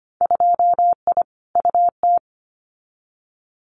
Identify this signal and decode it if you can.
Morse code, heard as a keyed tone, about 25 wpm, 690 Hz. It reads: 2I UT
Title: Winter Field Day Exchanges 25wpm